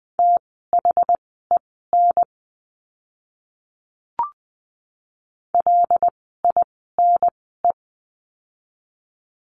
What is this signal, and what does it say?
Morse code, heard as a keyed tone, about 20 wpm, spaced as for 14 wpm, 700 Hz. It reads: THEN LINE
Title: 115 Lesson 19f - Letter w - Words 20wpm